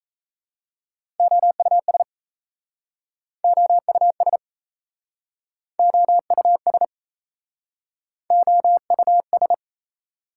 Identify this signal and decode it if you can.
Morse code, heard as a keyed tone, about 38 wpm, 700 Hz. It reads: OUS OUS OUS OUS